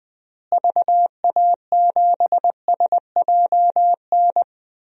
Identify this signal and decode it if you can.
Morse code, heard as a keyed tone, about 20 wpm, 700 Hz. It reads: VA7SJN
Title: DXCC-100 Call Signs - 6 Characters - Repeated 3x - 20wpm